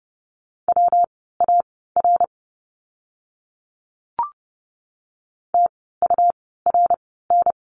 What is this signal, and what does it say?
Morse code, heard as a keyed tone, about 30 wpm, spaced as for 17 wpm, 700 Hz. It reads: WAR TURN
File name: course / 30:10wpm / 115 Lesson 19f - Letter w - Words 30wpm